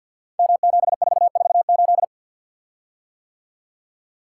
Morse code, heard as a keyed tone, about 50 wpm, 700 Hz. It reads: M7448
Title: Code Groups - 5 Letter:Numbers - 50wpm